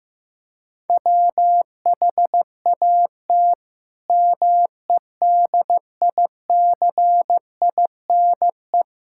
Morse code, heard as a keyed tone, about 16 wpm, 700 Hz. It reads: WHAT MEDICINE
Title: Sentences from Top 1000 Words - Review Part 6 - 15wpm